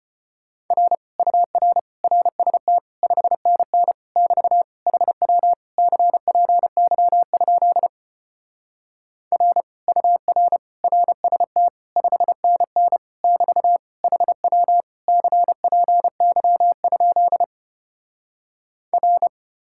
Morse code, heard as a keyed tone, about 33 wpm, 700 Hz. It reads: R UR RST 5NN <BT> HW CPY? R UR RST 5NN <BT> HW CPY? R